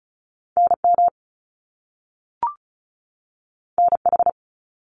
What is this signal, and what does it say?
Morse code, heard as a keyed tone, about 36 wpm, 700 Hz. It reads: NM NH
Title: US States 35wpm